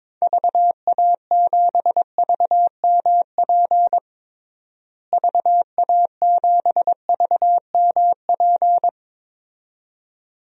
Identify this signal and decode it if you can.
Morse code, heard as a keyed tone, about 22 wpm, 700 Hz. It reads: VA7VMP VA7VMP
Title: DXCC-100 Call Signs - 6 Characters - Repeated 2x - 22wpm